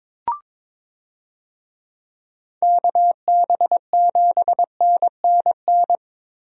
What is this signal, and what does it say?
Morse code, heard as a keyed tone, about 22 wpm, 700 Hz. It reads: KB7NNN
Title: USA Call Signs - 6 Characters - Repeated 2x - 22wpm